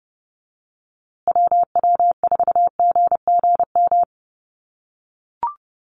Morse code, heard as a keyed tone, about 30 wpm, 700 Hz. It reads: WW4GGM
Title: USA Call Signs Repeated 2x 30wpm